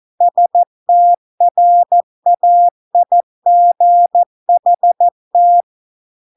Morse code, heard as a keyed tone, about 14 wpm, 700 Hz. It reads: STRAIGHT